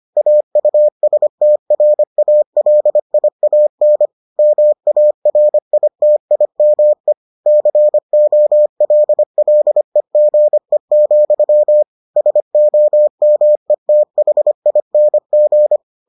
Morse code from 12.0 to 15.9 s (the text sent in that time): SOMETHING